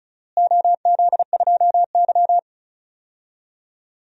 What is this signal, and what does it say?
Morse code, heard as a keyed tone, about 36 wpm, 700 Hz. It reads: OZ2Y
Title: DXCC-100 Call Signs - 4 Characters - Repeated 3x - 35wpm